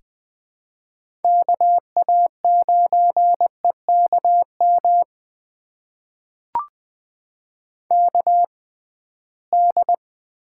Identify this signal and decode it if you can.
Morse code, heard as a keyed tone, about 20 wpm, spaced as for 15 wpm, 700 Hz. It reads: KA9EKM KD
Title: USA Call Signs - 6 Characters - 6x Character Spacing - 20wpm